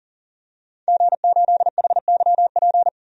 Morse code, heard as a keyed tone, about 40 wpm, 700 Hz. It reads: G8HYP
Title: DXCC-100 Call Signs - 5 Characters - 40wpm